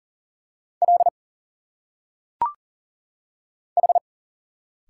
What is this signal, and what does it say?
Morse code, heard as a keyed tone, about 40 wpm, 700 Hz. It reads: L H
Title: Single Letters - Phonetic Pronunciation 40wpm